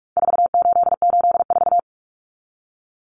Morse code, heard as a keyed tone, about 44 wpm, 700 Hz. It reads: 4884
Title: Common Cut Numbers - 4-Digit 45wpm